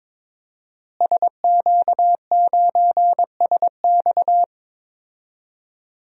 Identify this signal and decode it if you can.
Morse code, heard as a keyed tone, about 22 wpm, 700 Hz. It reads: SQ9SX
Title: DXCC-100 Call Signs - 5 Characters - Repeated 3x - 22wpm